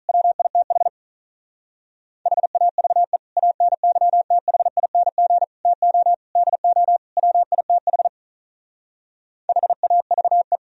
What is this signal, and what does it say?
Morse code, heard as a keyed tone, about 43 wpm, 700 Hz. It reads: WITH HAVE ANYTHING TO DO WITH HAVE
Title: Speed-Racing - Sets of 5 Words 35wpm